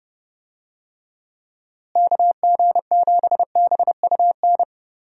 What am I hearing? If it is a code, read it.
Morse code, heard as a keyed tone, about 30 wpm, 700 Hz. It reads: KG7BUN
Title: USA Call Signs - 6 Characters - Repeated 2x - 30wpm